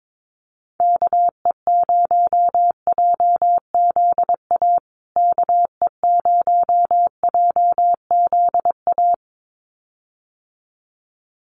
Morse code, heard as a keyed tone, about 22 wpm, 700 Hz. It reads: KE0JZA KE0JZA
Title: USA Call Signs Repeated 2x 22wpm